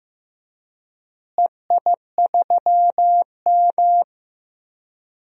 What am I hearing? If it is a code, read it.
Morse code, heard as a keyed tone, about 15 wpm, 700 Hz. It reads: EI3M
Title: DXCC-100 Call Signs - 4 Characters - Repeated 3x - 15wpm